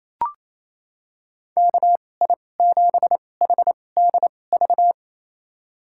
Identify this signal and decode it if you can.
Morse code, heard as a keyed tone, about 27 wpm, spaced as for 20 wpm, 700 Hz. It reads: KI7HDV
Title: USA Call Signs - 2x Character Spacing - 28wpm